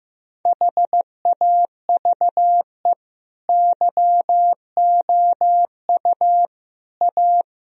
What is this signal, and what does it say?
Morse code, heard as a keyed tone, about 16 wpm, 700 Hz. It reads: HAVE YOU A
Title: Sentences from Top 1000 Words - Review Part 1 - 15wpm